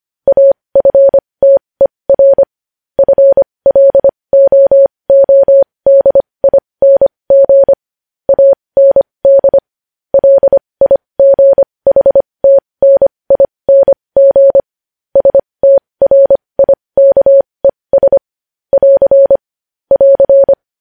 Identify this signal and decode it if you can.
Morse code, heard as a keyed tone, about 25 wpm, spaced as for 20 wpm, 550 Hz. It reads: AFTER FLOODING AND LIGHTNING STRIKES <AR> <AR>
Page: News headlines in Morse code at 20 WPM